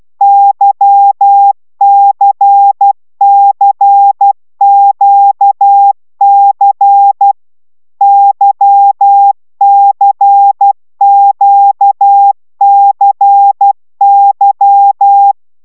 Morse code, heard as a keyed tone, about 12 wpm, 800 Hz. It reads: YCCQC YCQCY
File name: taviro tanulas / cw 75 15 10